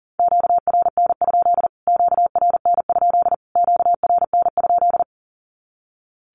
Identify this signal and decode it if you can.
Morse code, heard as a keyed tone, about 40 wpm, 700 Hz. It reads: QRN? QRN? QRN?
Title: Q-Codes Repeated 3x 40wpm